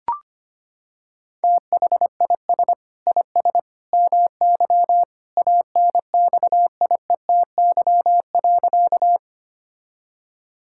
Morse code, heard as a keyed tone, about 25 wpm, 700 Hz. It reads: THIS IS MY ANXIETY.
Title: Sentences from Top 1600 Words 25wpm